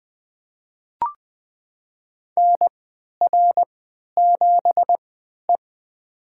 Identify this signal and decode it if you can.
Morse code, heard as a keyed tone, about 20 wpm, spaced as for 11 wpm, 700 Hz. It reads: NR7E
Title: USA Call Signs - 4 Characters - 3x Character Spacing - 20wpm